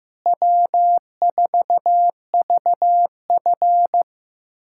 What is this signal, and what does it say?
Morse code, heard as a keyed tone, about 15 wpm, 700 Hz. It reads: W4VF
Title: USA Callsign DE Callsign - 4 Characters - 15wpm